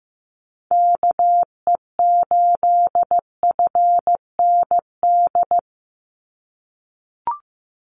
Morse code, heard as a keyed tone, about 15 wpm, 700 Hz. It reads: KE8FND
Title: USA Call Signs 15wpm